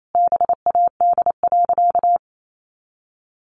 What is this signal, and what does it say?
Morse code, heard as a keyed tone, about 28 wpm, 700 Hz. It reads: BAD.
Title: Sentences from Top 1500 Words 28wpm